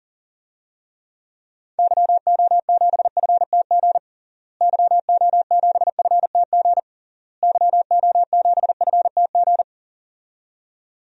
Morse code, heard as a keyed tone, about 40 wpm, 700 Hz. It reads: YO7FTG YO7FTG YO7FTG
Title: DXCC-100 Call Signs - 6 Characters - Repeated 3x - 40wpm